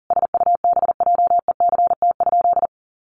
Morse code, heard as a keyed tone, about 40 wpm, 700 Hz. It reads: SUBJECT?
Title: Sentences from Top 1200 Words 40wpm